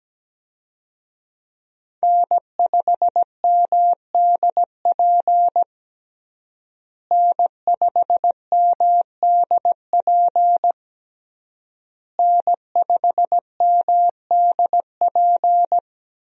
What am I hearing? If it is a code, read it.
Morse code, heard as a keyed tone, about 17 wpm, 700 Hz. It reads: N5MDP N5MDP N5MDP
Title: USA Call Signs - 5 Characters - Repeated 3x - 17wpm